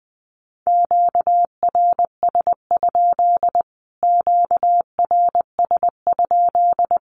Morse code, heard as a keyed tone, about 20 wpm, 700 Hz. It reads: QRS? QRS?
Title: Q-Codes Repeated 3x 20wpm